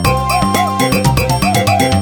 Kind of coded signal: none